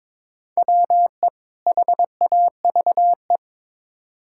Morse code, heard as a keyed tone, about 22 wpm, 700 Hz. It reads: WE HAVE
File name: Build A Phrase - 2-Word Phrases - 22wpm